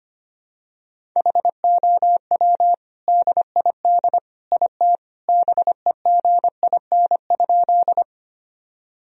Morse code, heard as a keyed tone, about 25 wpm, 700 Hz. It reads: HOW DID IT BEGIN?
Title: Sentences from Top 1000 Words - Review Part 1 - 25wpm